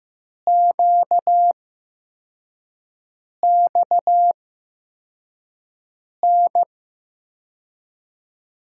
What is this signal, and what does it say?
Morse code, heard as a keyed tone, about 15 wpm, spaced as for 4 wpm, 700 Hz. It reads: QXN
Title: USA Call Signs - 5 Characters - 8x Character Spacing - 15wpm